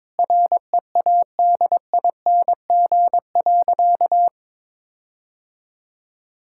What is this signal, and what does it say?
Morse code, heard as a keyed tone, about 22 wpm, 700 Hz. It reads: READING.
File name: practice / 22wpm / Sentences with 8-Word Phrases - 22wpm